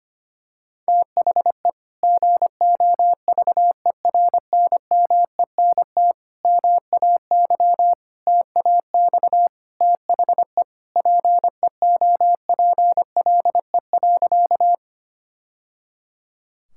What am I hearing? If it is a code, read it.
Morse code, heard as a keyed tone, about 25 wpm, 700 Hz. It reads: THE GOVERNMENT MAY TAX THE PEOPLE.
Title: Sentences from Top 1000 Words - Review Part 5 - 25wpm